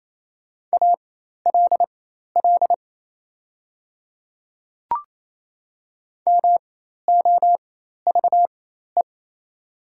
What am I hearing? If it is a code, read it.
Morse code, heard as a keyed tone, about 27 wpm, spaced as for 13 wpm, 690 Hz. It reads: ALL MOVE
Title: 4x Character Spacing - Top 200 Words - 28wpm